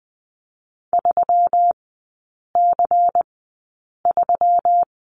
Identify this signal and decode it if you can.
Morse code, heard as a keyed tone, about 20 wpm, spaced as for 8 wpm, 700 Hz. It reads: 3C3
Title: 121 Lesson 20e - Number 3 - Three Characters 20wpm